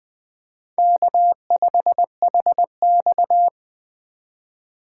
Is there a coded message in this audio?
Yes: Morse code from 0.8 to 3.5 s, heard as a keyed tone, about 20 wpm, 700 Hz. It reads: K5HX